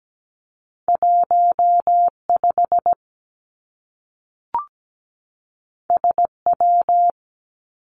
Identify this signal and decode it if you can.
Morse code, heard as a keyed tone, about 17 wpm, 700 Hz. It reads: 1 5 S W
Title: Two Challenging Letter-Number Combinations 17wpm